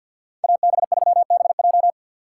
Morse code, heard as a keyed tone, about 62 wpm, 700 Hz. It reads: AB3BJ
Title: Speed Racing - USA 2x2 Call Signs 50wpm